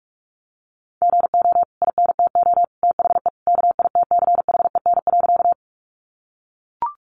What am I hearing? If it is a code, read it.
Morse code, heard as a keyed tone, about 44 wpm, 700 Hz. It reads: GO INTO THE KITCHEN.
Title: Sentences from Top 1400 Words 45wpm